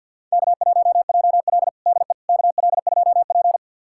Morse code, heard as a keyed tone, about 52 wpm, 700 Hz. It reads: K1JL DE KL2P
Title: USA Callsign DE Callsign - 4 Characters - 50wpm